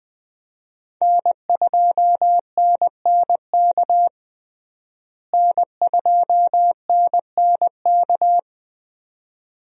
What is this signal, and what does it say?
Morse code, heard as a keyed tone, about 20 wpm, 700 Hz. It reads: N2NNK N2NNK
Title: USA Call Signs - 5 Characters - Repeated 2x - 20wpm